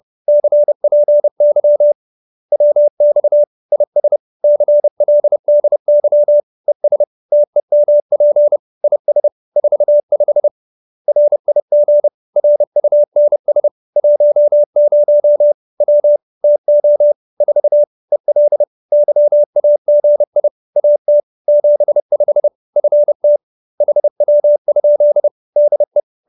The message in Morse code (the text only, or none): CPY WX IS CLDY ES TEMP IS 45 RIG RUNS 10 W TO 4 EL YAGI AT 75 FT HW? DE